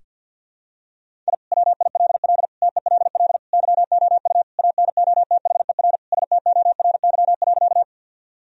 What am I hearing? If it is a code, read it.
Morse code, heard as a keyed tone, about 46 wpm, 700 Hz. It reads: I WILL TELL YOU ANOTHER STORY.
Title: Sentences from Top 1000 Words - Review Part 1 - 50wpm